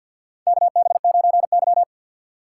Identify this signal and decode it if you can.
Morse code, heard as a keyed tone, about 48 wpm, 690 Hz. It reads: KB9Y
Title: USA Call Signs - 6x Character Spacing - 50wpm